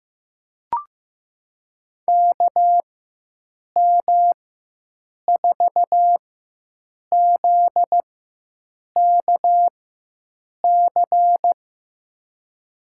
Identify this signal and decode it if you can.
Morse code, heard as a keyed tone, about 15 wpm, spaced as for 7 wpm, 700 Hz. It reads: KM4ZKC
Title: USA Call Signs - 6 Characters - 4x Character Spacing - 15wpm